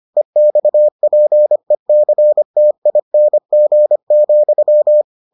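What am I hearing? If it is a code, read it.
Morse code, heard as a keyed tone, about 25 wpm, 600 Hz. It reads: EXPECTING,
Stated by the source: Morse edition of the Foundations of Amateur Radio episodes.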